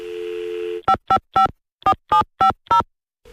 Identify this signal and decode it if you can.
DTMF, heard as telephone touch tones, tones of about 110 ms. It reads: 5554750